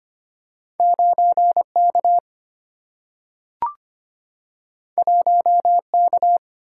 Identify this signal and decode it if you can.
Morse code, heard as a keyed tone, about 25 wpm, 700 Hz. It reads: 9K 1K
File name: Two Letter-Number Combinations - 25wpm